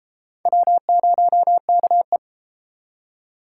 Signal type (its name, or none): Morse code